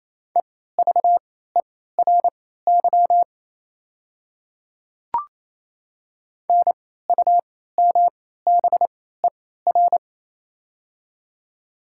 Morse code, heard as a keyed tone, about 27 wpm, spaced as for 16 wpm, 690 Hz. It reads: EVERY NUMBER
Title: 3x Character Spacing - Top 300 Words - 28wpm